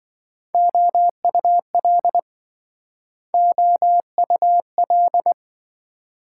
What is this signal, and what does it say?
Morse code, heard as a keyed tone, about 28 wpm, 700 Hz. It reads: OUL OUL